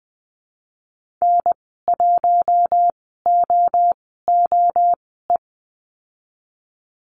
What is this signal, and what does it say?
Morse code, heard as a keyed tone, about 20 wpm, spaced as for 15 wpm, 700 Hz. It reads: N1OOE
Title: 042 Lesson 8g - Number 1 - Call Signs 20wpm